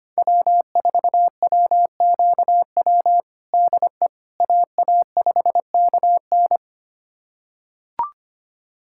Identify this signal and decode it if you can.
Morse code, heard as a keyed tone, about 25 wpm, 700 Hz. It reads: W4WQW DE AA5KN
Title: USA Callsign DE Callsign - 5 Characters - 25wpm